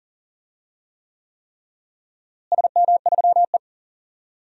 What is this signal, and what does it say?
Morse code, heard as a keyed tone, about 40 wpm, 700 Hz. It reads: SM3E
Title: DXCC-100 Call Signs - 4 Characters - Repeated 3x - 40wpm